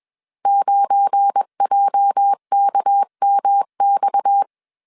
Morse code, heard as a keyed tone, about 21 wpm, 800 Hz. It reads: 9JKMX